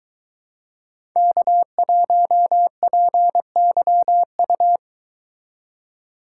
Morse code, heard as a keyed tone, about 22 wpm, 700 Hz. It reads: K1PYU